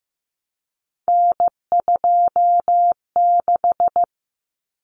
Morse code, heard as a keyed tone, about 14 wpm, 700 Hz. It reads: N26
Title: Common Cut Numbers - 3-Digit 15wpm